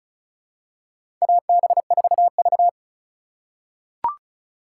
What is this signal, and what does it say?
Morse code, heard as a keyed tone, about 36 wpm, 700 Hz. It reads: AB4V
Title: USA Call Signs - 4 Characters - 2x Character Spacing - 35wpm